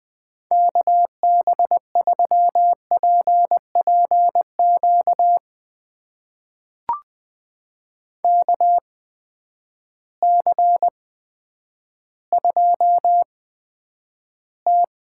Morse code, heard as a keyed tone, about 20 wpm, spaced as for 10 wpm, 700 Hz. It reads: KB3PPQ KC2T
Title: USA Call Signs - 6 Characters - 8x Character Spacing - 20wpm